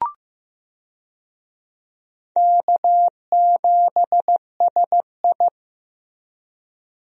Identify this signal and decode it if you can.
Morse code, heard as a keyed tone, about 14 wpm, 700 Hz. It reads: K7SI
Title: USA Call Signs - 4 Characters - Repeated 2x - 15wpm